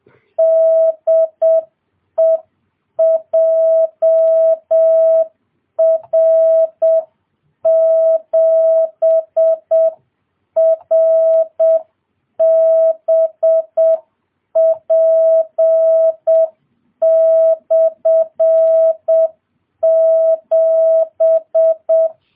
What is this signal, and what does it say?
Morse code, heard as a keyed tone, about 7 wpm, 650 Hz. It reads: DE JR7RBP/7